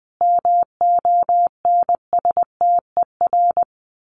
Morse code, heard as a keyed tone, about 19 wpm, 700 Hz. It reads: MONSTER
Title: A Sound Of Thunder 20wpm